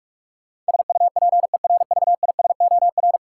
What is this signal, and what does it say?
Morse code, heard as a keyed tone, about 46 wpm, 700 Hz. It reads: SUPERVISOR
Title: Practice QSO - Exchange 3 - More Info - 45wpm